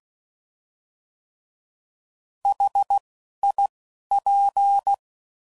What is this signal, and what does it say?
Morse code, heard as a keyed tone, about 16 wpm, spaced as for 11 wpm, 800 Hz. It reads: HIP
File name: Mixed / Contrasts / 3LtrContrasts